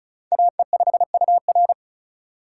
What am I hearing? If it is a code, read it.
Morse code, heard as a keyed tone, about 36 wpm, 700 Hz. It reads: AE5UR
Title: USA Call Signs - 5 Characters - 3x Character Spacing - 35wpm